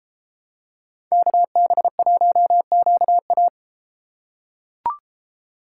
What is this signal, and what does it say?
Morse code, heard as a keyed tone, about 31 wpm, 700 Hz. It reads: KB1QA